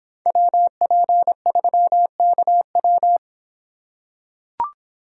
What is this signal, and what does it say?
Morse code, heard as a keyed tone, about 26 wpm, 700 Hz. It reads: WP3KW